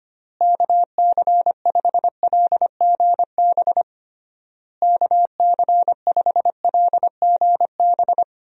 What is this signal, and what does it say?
Morse code, heard as a keyed tone, about 25 wpm, 700 Hz. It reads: KC5LGB KC5LGB
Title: USA Call Signs - 6 Characters - Repeated 3x - 25wpm